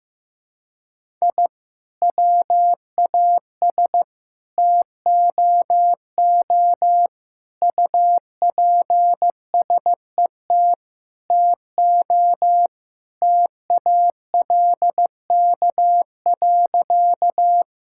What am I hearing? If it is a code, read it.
Morse code, heard as a keyed tone, about 16 wpm, 700 Hz. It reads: I WAS TOO UPSET TO TALK.
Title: Sentences from Top 1000 Words - Review Part 5 - 15wpm